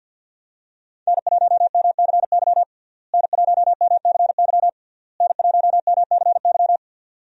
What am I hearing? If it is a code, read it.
Morse code, heard as a keyed tone, about 52 wpm, 700 Hz. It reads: N1MCY N1MCY N1MCY
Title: USA Call Signs - 5 Characters - Repeated 3x - 50wpm